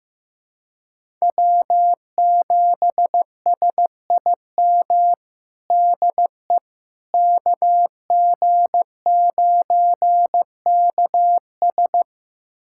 Morse code, heard as a keyed tone, about 15 wpm, 700 Hz. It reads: W7SIM DE KG9KS
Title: USA Callsign DE Callsign - 5 Characters - 15wpm